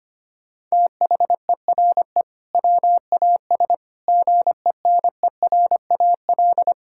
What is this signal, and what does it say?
Morse code, heard as a keyed tone, about 25 wpm, 700 Hz. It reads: THERE WAS GENERAL
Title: Sentences from Top 1600 Words 25wpm